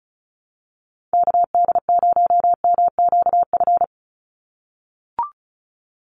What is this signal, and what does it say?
Morse code, heard as a keyed tone, about 34 wpm, 700 Hz. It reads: KD0MQF
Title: USA Call Signs 35wpm